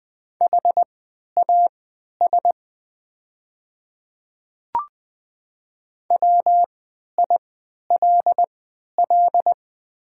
Morse code, heard as a keyed tone, about 20 wpm, spaced as for 11 wpm, 700 Hz. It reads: HAS WILL
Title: 3x Character Spacing - Top 100 Words - 20wpm